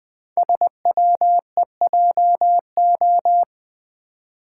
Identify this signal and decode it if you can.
Morse code, heard as a keyed tone, about 20 wpm, 700 Hz. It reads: SWEJO
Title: Code Groups - 5 Letters - 20wpm